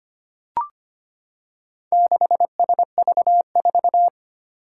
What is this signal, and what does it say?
Morse code, heard as a keyed tone, about 25 wpm, 700 Hz. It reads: 6SV4
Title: Four Challenging Letter-Number Combinations 25wpm